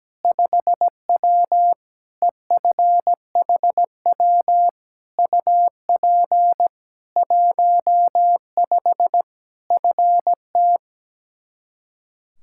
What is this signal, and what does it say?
Morse code, heard as a keyed tone, about 17 wpm, 700 Hz. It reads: 5W EFHW UP 15 FT
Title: Practice QSO - Exchange 2 - Info - 17wpm